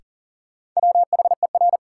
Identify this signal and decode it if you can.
Morse code, heard as a keyed tone, about 39 wpm, 700 Hz. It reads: WHER
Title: Sentences from Top 1500 Words - New Word Drill Alt - 40wpm